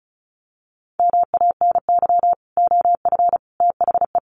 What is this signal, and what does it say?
Morse code, heard as a keyed tone, about 34 wpm, 700 Hz. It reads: MANY OF THE
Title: Sets of 3 Words 35wpm